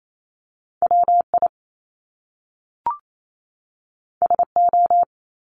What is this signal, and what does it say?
Morse code, heard as a keyed tone, about 28 wpm, 700 Hz. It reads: WI SO
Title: Top Two Letter Combinations in Words 28wpm